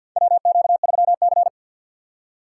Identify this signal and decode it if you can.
Morse code, heard as a keyed tone, about 53 wpm, 700 Hz. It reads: WQ3C
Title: USA Call Signs - 4 Characters - 3x Character Spacing - 50wpm